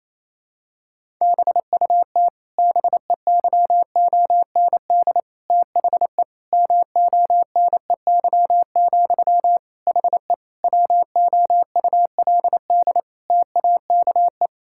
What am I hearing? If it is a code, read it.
Morse code, heard as a keyed tone, about 28 wpm, 700 Hz. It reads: BUT BEYOND THE MONEY, HE WOULD TAKE
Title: Sentences from Top 1000 Words - Review Part 6 - 28wpm